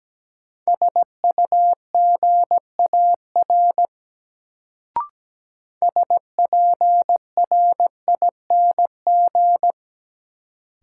Morse code, heard as a keyed tone, about 17 wpm, 700 Hz. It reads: SUGAR SPRING
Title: Sentences from Top 1200 Words - New Word Drill Alt - 17wpm